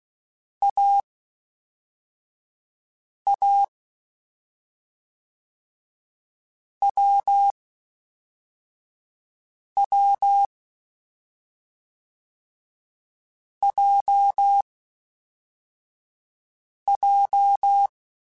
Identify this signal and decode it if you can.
Morse code, heard as a keyed tone, about 16 wpm, spaced as for 4 wpm, 800 Hz. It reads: AA WW JJ
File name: Difficult Combos / Difficult Combos-mp3s / AWJ1 @5